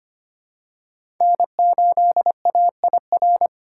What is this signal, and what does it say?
Morse code, heard as a keyed tone, about 25 wpm, 700 Hz. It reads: N8AIR
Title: USA Call Signs - 5 Characters - 2x Character Spacing - 25wpm